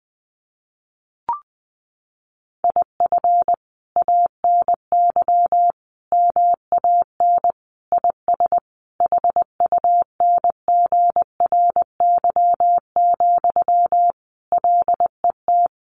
Morse code, heard as a keyed tone, about 20 wpm, 700 Hz. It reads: IF ANY MAN IS HUNGRY, LET
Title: Sentences from Top 1400 Words 20wpm